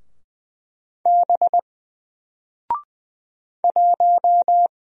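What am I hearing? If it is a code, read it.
Morse code, heard as a keyed tone, about 20 wpm, 700 Hz. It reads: B 1
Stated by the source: Single letter and number Morse drill at 20 wpm, from N4LCD.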